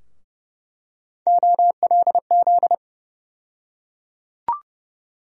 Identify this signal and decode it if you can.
Morse code, heard as a keyed tone, about 31 wpm, 700 Hz. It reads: OLZ
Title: USA Call Sign Postfixes - 3 Letters 30wpm